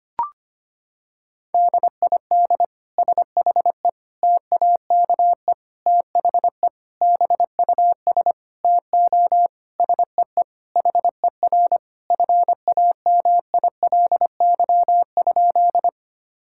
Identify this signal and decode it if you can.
Morse code, heard as a keyed tone, about 25 wpm, 700 Hz. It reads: DID SHE TAKE THE BUS TO SEE HER FAMILY?
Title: Sentences from Top 1000 Words - Review Part 6 - 25wpm